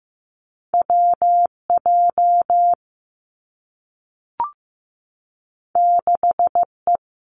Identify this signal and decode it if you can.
Morse code, heard as a keyed tone, about 15 wpm, 700 Hz. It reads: WJ 6E
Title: Two Challenging Letter-Number Combinations 15wpm